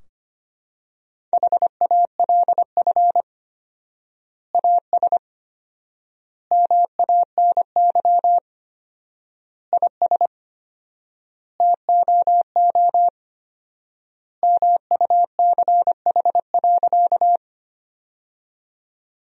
Morse code, heard as a keyed tone, about 25 wpm, 700 Hz. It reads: HALF AS MANY IS TOO MUCH.
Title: 4x Word-Spacing - Sentences from Top 400 Words - 25wpm